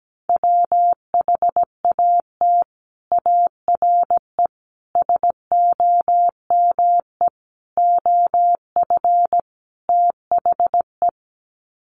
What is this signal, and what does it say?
Morse code, heard as a keyed tone, about 17 wpm, 700 Hz. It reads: WHAT ARE SOME OF THE
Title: Sets of 5 Words Encore 17wpm